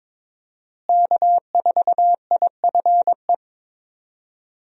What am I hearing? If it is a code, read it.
Morse code, heard as a keyed tone, about 22 wpm, 700 Hz. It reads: K4IFE
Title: USA Call Signs - 5 Characters - Repeated 2x - 22wpm